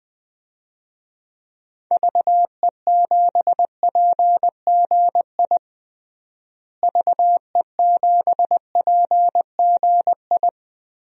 Morse code, heard as a keyed tone, about 20 wpm, 700 Hz. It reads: VE7PGI VE7PGI
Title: DXCC-100 Call Signs - 6 Characters - Repeated 2x - 20wpm